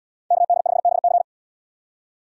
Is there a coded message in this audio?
Yes: Morse code from 0.3 to 1.2 s, heard as a keyed tone, about 76 wpm, 700 Hz. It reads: KD5BC